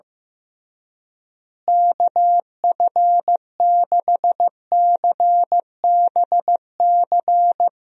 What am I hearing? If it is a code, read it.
Morse code, heard as a keyed tone, about 15 wpm, 700 Hz. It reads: KF6CBC
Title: USA Call Signs - 6 Characters - Repeated 2x - 15wpm